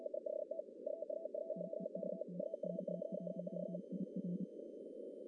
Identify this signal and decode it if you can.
Morse code, heard as a keyed tone, about 29 wpm, 180 Hz. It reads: TEST K1IR
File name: CW / cqww-cw-k5tr